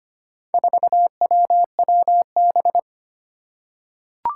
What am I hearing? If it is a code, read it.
Morse code, heard as a keyed tone, about 25 wpm, 700 Hz. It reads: 4WWB
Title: Four Challenging Letter-Number Combinations 25wpm